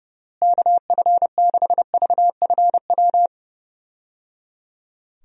Morse code, heard as a keyed tone, about 30 wpm, 700 Hz. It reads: KF6VFW
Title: USA Call Signs - 6 Characters - Repeated 3x - 30wpm